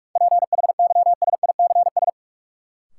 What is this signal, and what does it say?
Morse code, heard as a keyed tone, about 45 wpm, 700 Hz. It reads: PHYSICS
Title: Sentences from Top 1500 Words - New Word Drill Alt - 45wpm